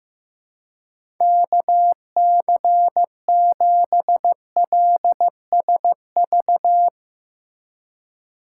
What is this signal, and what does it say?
Morse code, heard as a keyed tone, about 15 wpm, 700 Hz. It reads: KC7LSV
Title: USA Call Signs - 6 Characters - 4x Character Spacing - 15wpm